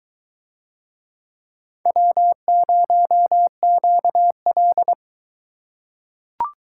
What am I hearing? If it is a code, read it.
Morse code, heard as a keyed tone, about 22 wpm, 700 Hz. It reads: W0QL